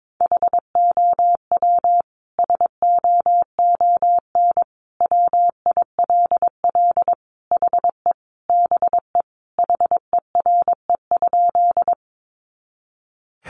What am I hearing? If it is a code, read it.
Morse code, heard as a keyed tone, about 22 wpm, 700 Hz. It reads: HOW SOON WILL HE BE HERE?
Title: Sentences from Top 400 Words 22wpm